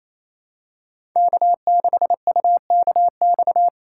Morse code, heard as a keyed tone, about 27 wpm, 700 Hz. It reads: K6UKX
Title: USA Call Signs - 5 Characters - 2x Character Spacing - 28wpm